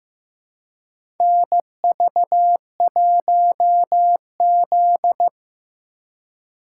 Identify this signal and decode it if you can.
Morse code, heard as a keyed tone, about 15 wpm, 700 Hz. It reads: NV1Z
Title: USA Call Signs - 4 Characters - 2x Character Spacing - 15wpm